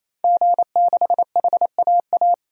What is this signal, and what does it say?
Morse code, heard as a keyed tone, about 28 wpm, 700 Hz. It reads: G6HAA
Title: DXCC-100 Call Signs - 5 Characters - 28wpm